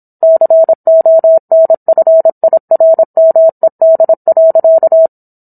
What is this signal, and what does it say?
Morse code, heard as a keyed tone, about 25 wpm, 650 Hz. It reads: CONFIRMED.